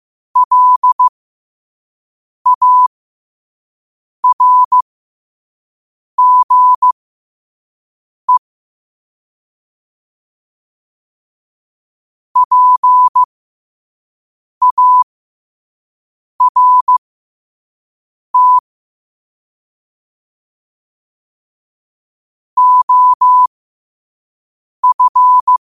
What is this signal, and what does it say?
Morse code, heard as a keyed tone, about 15 wpm, spaced as for 5 wpm, 1000 Hz. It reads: LARGE PART OF
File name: qotdfiles / 5wpm